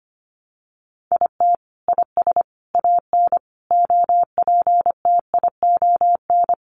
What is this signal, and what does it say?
Morse code, heard as a keyed tone, about 25 wpm, 700 Hz. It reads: IT IS AN OPTION
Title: Sentences from Top 1300 Words 25wpm